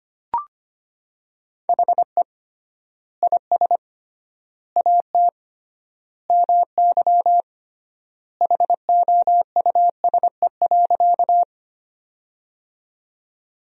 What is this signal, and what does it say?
Morse code, heard as a keyed tone, about 25 wpm, 700 Hz. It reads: HE IS AT MY HOUSE.
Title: 3x Word-Spacing - Sentences from Top 200 Words - 25wpm